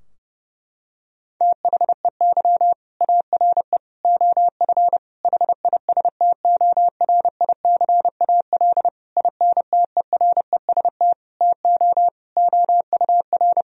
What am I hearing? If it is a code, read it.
Morse code, heard as a keyed tone, about 32 wpm, 700 Hz. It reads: THEY ARE OF HISTORICAL INTEREST TO OUR
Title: Sentences from Top 1000 Words - Review Part 4 - 30wpm